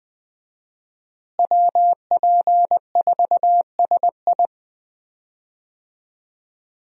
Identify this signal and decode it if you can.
Morse code, heard as a keyed tone, about 20 wpm, 700 Hz. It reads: WP4SI
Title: USA Call Signs - 5 Characters - Repeated 3x - 20wpm